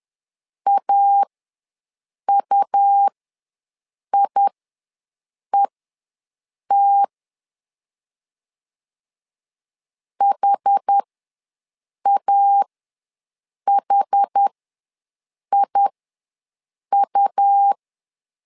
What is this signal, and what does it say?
Morse code, heard as a keyed tone, about 10 wpm, spaced as for 6 wpm, 800 Hz. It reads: AUIET HAHIU
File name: LEZIONI / 2 UHDN / 2 UH ws 10wpm 2029